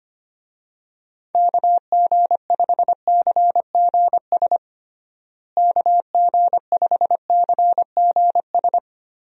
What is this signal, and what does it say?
Morse code, heard as a keyed tone, about 25 wpm, 700 Hz. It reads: KG5CGS KG5CGS
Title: USA Call Signs - 6 Characters - Repeated 3x - 25wpm